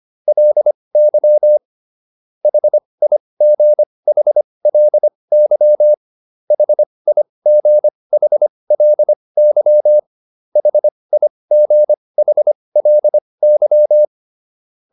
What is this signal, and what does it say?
Morse code, heard as a keyed tone, about 25 wpm, spaced as for 20 wpm, 600 Hz. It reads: LY HIGHLY HIGHLY HIGHLY